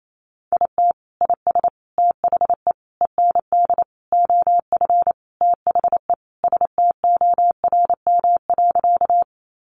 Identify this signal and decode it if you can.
Morse code, heard as a keyed tone, about 28 wpm, 700 Hz. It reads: IT IS THE END OF THE STORM.
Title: Sentences from Top 1500 Words 28wpm